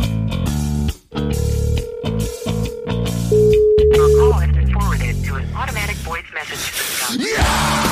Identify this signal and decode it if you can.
Ringback tone (Europe, 425 Hz), with speech.